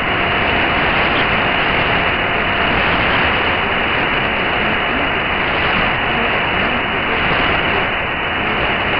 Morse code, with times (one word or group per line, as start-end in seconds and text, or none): none